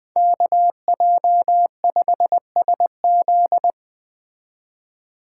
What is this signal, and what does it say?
Morse code, heard as a keyed tone, about 21 wpm, 700 Hz. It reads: KJ5SZ
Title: Speed Racing - USA Call Signs 17wpm